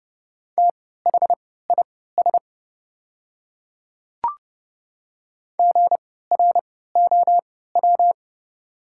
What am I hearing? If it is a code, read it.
Morse code, heard as a keyed tone, about 30 wpm, spaced as for 17 wpm, 700 Hz. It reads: THIS GROW
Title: 3x Character Spacing - Top 500 Words - 30wpm